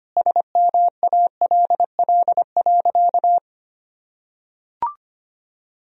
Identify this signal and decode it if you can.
Morse code, heard as a keyed tone, about 25 wpm, 700 Hz. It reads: SMALL.
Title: Sentences from Top 1000 Words - Review Part 4 - 25wpm